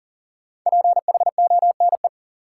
Morse code, heard as a keyed tone, about 40 wpm, 700 Hz. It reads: PHONE
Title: Sentences from Top 1100 Words - New Word Drill Alt - 40wpm